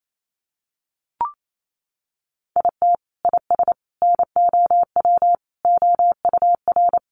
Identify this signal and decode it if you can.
Morse code, heard as a keyed tone, about 28 wpm, 700 Hz. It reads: IT IS NOW OUR
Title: Sentences from Top 1400 Words 28wpm